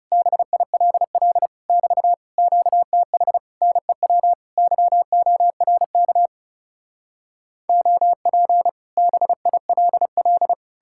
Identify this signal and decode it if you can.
Morse code, heard as a keyed tone, about 40 wpm, 700 Hz. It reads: BILL <BT> QTH NEW YORK OP BILL